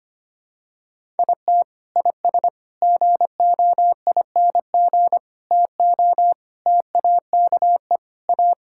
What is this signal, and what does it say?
Morse code, heard as a keyed tone, about 25 wpm, 700 Hz. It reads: IT IS GOING TO TAKE A
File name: practice / 25wpm / Sentences with 4-Word Phrases - 25wpm